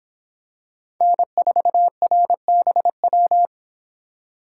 Morse code, heard as a keyed tone, about 25 wpm, 700 Hz. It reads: N4RBW